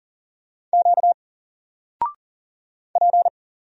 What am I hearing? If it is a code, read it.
Morse code, heard as a keyed tone, about 40 wpm, 700 Hz. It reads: Q P
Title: Single Character (Letter:Number:Punctuation:Prosign) 40wpm